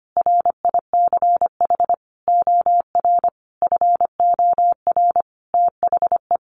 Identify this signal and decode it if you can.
Morse code, heard as a keyed tone, about 25 wpm, 700 Hz. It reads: RICH OR FOR THE
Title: Sentences from Top 1100 Words 25wpm